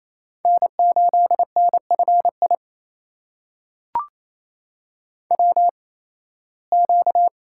Morse code, heard as a keyed tone, about 28 wpm, spaced as for 18 wpm, 700 Hz. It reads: N8NFI WQ
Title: USA Call Signs - 8x Character Spacing - 28wpm